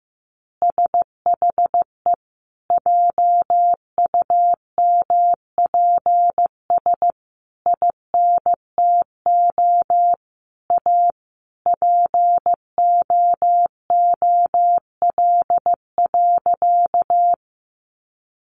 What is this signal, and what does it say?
Morse code, heard as a keyed tone, about 15 wpm, 700 Hz. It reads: SHE JUMPS INTO A POOL.
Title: Additional Easy Phrases 15wpm